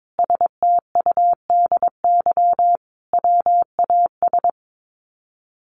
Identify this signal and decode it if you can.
Morse code, heard as a keyed tone, about 22 wpm, 700 Hz. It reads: STUDY WAS
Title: Sets of 5 Words 22wpm